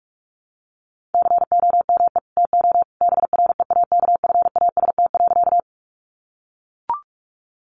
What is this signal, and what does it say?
Morse code, heard as a keyed tone, about 44 wpm, 700 Hz. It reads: COME TO BREAKFAST.
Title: Sentences from Top 1500 Words 45wpm